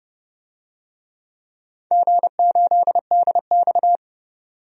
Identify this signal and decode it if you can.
Morse code, heard as a keyed tone, about 30 wpm, 700 Hz. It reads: G8DX
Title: DXCC-100 Call Signs - 4 Characters - 30wpm